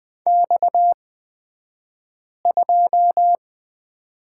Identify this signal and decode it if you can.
Morse code, heard as a keyed tone, about 20 wpm, spaced as for 5 wpm, 700 Hz. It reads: X2
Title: 242 Lesson 39d - Letter z - Two Characters 20wpm